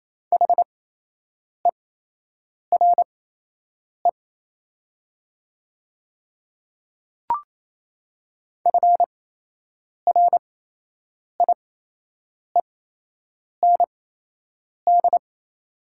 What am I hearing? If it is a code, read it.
Morse code, heard as a keyed tone, about 28 wpm, spaced as for 8 wpm, 700 Hz. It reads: HERE FRIEND
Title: 8x Character Spacing - Top 300 Words - 28wpm